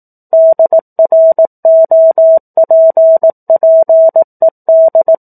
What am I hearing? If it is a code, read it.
Morse code, heard as a keyed tone, about 18 wpm, 650 Hz. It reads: DROPPED
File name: 04-Chapter-04 18 WPM